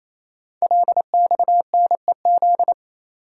Morse code, heard as a keyed tone, about 28 wpm, 690 Hz. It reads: LXNEZ
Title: Code Groups - 5 Letters - 28wpm